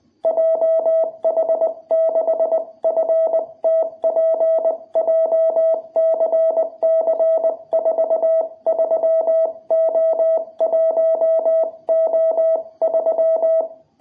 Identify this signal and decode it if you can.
Morse code, heard as a keyed tone, about 20 wpm, spaced as for 18 wpm, 660 Hz. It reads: JH6FTP JCC43O1O3
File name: jh8cbh / 2021cw